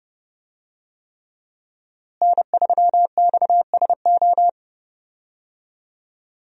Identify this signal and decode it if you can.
Morse code, heard as a keyed tone, about 30 wpm, 700 Hz. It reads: N3XSO
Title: USA Call Signs - 5 Characters - Repeated 3x - 30wpm